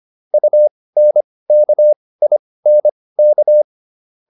Morse code, heard as a keyed tone, about 25 wpm, spaced as for 18 wpm, 600 Hz. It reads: UNKINK